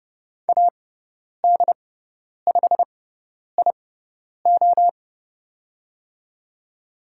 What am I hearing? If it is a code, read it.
Morse code, heard as a keyed tone, about 30 wpm, spaced as for 10 wpm, 700 Hz. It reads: AD5IO
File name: Lesson 16g - Letter c - Call Signs 30wpm